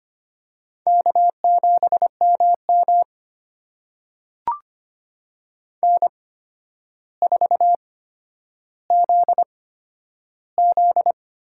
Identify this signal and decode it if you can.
Morse code, heard as a keyed tone, about 25 wpm, spaced as for 11 wpm, 700 Hz. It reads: K7MM N4ZZ
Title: USA Call Signs - 4 Characters - 8x Character Spacing - 25wpm